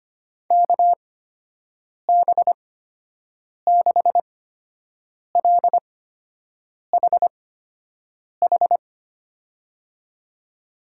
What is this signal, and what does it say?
Morse code, heard as a keyed tone, about 25 wpm, spaced as for 7 wpm, 700 Hz. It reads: KB6LHH
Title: USA Call Signs - 6 Characters - 8x Character Spacing - 25wpm